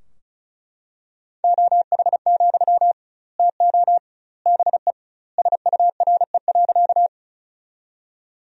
Morse code, heard as a keyed tone, about 34 wpm, 700 Hz. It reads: OH, TO BE SURE.
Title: 2x Word-Spacing - Sentences from Top 500 Words - 35wpm